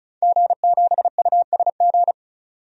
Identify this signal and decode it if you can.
Morse code, heard as a keyed tone, about 34 wpm, 700 Hz. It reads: G7USG
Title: DXCC-100 Call Signs - 5 Characters - 35wpm